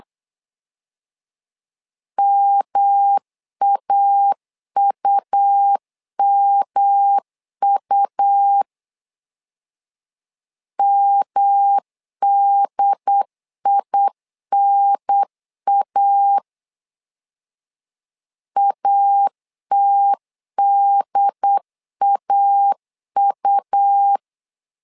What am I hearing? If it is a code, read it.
Morse code, heard as a keyed tone, about 8 wpm, 800 Hz. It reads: MAUMU MDINA ATDAU
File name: LEZIONI / 3 MWRO / 3 M 8wpm 2300